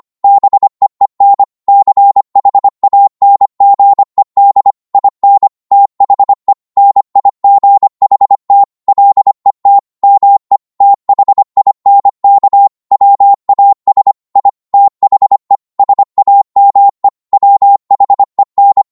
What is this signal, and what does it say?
Morse code, heard as a keyed tone, about 25 wpm, 800 Hz. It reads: BEEN CHANGED IN THE NIGHT LET ME THINK WAS I THE SAME WHEN